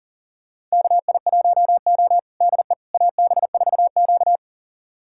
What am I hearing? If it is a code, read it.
Morse code, heard as a keyed tone, about 40 wpm, 700 Hz. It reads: KI1O DE AB4Q
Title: USA Callsign DE Callsign - 4 Characters - 40wpm